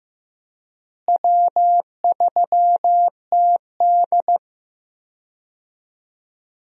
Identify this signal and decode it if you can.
Morse code, heard as a keyed tone, about 15 wpm, 700 Hz. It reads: W3TD
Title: USA Call Signs - 4 Characters - Repeated 3x - 15wpm